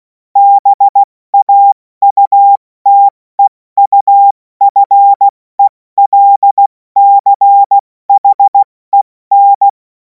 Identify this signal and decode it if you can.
Morse code, heard as a keyed tone, about 16 wpm, 800 Hz. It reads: BAUTEUFELCHEN
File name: Morsekurs / Witze / Witz 03. März 2026 70 F80